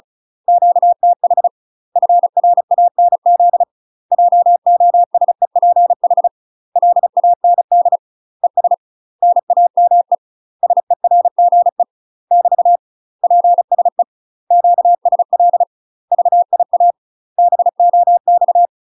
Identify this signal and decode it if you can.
Morse code, heard as a keyed tone, about 35 wpm, 700 Hz. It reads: QTH FRANZ JOSEPH LAND ES NAME SERGE <BT> PSE QSL VIA BOX
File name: UK MASTER QSOs TRAINING / 35WPM / UK MASTER QSOs 0023